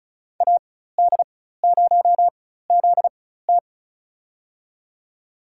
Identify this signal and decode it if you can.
Morse code, heard as a keyed tone, about 34 wpm, spaced as for 16 wpm, 700 Hz. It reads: AD0ZT
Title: USA Call Signs - 4x Character Spacing - 35wpm